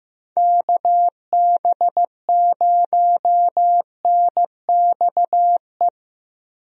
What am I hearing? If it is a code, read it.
Morse code, heard as a keyed tone, about 15 wpm, 700 Hz. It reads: KB0NXE
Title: USA Call Signs - 6 Characters - Repeated 3x - 15wpm